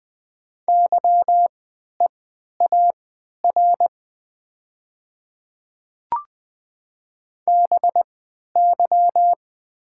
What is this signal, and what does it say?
Morse code, heard as a keyed tone, about 20 wpm, spaced as for 11 wpm, 700 Hz. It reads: YEAR BY
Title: 3x Character Spacing - Top 300 Words - 20wpm